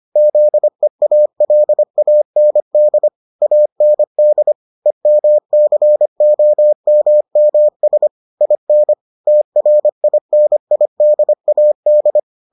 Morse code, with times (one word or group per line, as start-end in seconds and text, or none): 0.2-3.1 ZEALAND
3.4-4.5 AND
4.9-8.1 EMCOMMS
8.4-8.9 IN
9.3-12.2 TRINIDAD